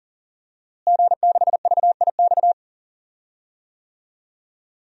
Morse code, heard as a keyed tone, about 40 wpm, 700 Hz. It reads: G6VIX
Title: DXCC-100 Call Signs - 5 Characters - Repeated 2x - 40wpm